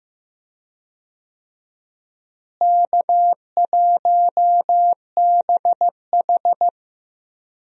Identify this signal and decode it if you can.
Morse code, heard as a keyed tone, about 15 wpm, 700 Hz. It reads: K1BH
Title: USA Call Signs - 4 Characters - Repeated 2x - 15wpm